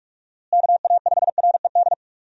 Morse code, heard as a keyed tone, about 46 wpm, 700 Hz. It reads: KA5RED
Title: USA Call Signs - 6 Characters - Repeated 3x - 45wpm